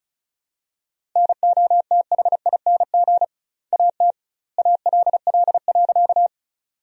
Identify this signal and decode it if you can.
Morse code, heard as a keyed tone, about 34 wpm, 700 Hz. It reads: NOTHING AT ALL.
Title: 2x Word-Spacing - Sentences from Top 500 Words - 35wpm